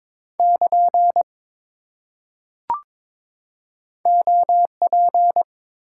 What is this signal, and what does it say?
Morse code, heard as a keyed tone, about 22 wpm, 700 Hz. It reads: <KN> OP
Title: Top 90 QSO Elements 22wpm